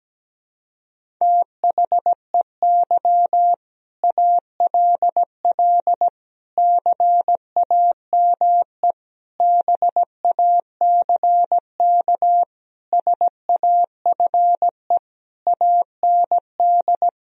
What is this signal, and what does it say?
Morse code, heard as a keyed tone, about 17 wpm, 700 Hz. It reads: THEY ALL CAME BACK SAFE AND
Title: Sentences from Top 1000 Words - Review Part 1 - 17wpm